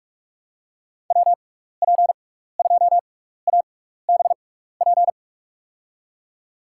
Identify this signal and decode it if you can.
Morse code, heard as a keyed tone, about 46 wpm, spaced as for 15 wpm, 700 Hz. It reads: WP2ABP
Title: USA Call Signs - 6 Characters - 6x Character Spacing - 45wpm